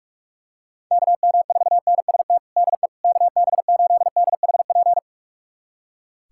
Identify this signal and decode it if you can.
Morse code, heard as a keyed tone, about 42 wpm, 700 Hz. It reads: KM4NST DE KB8DHP
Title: USA Callsign DE Callsign 45wpm